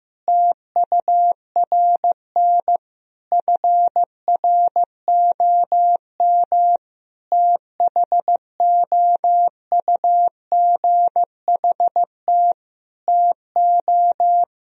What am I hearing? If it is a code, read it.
Morse code, heard as a keyed tone, about 15 wpm, 700 Hz. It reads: TURN FROM THOUGHT TO
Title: Sentences from Top 1000 Words - Review Part 4 - 15wpm